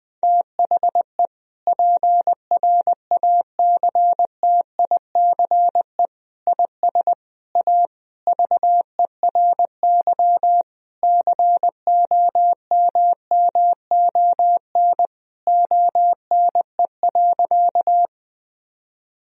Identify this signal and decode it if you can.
Morse code, heard as a keyed tone, about 20 wpm, 700 Hz. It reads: THE PRACTICE IS A VERY COMMON ONE.
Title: Sentences from Top 1000 Words - Review Part 4 - 20wpm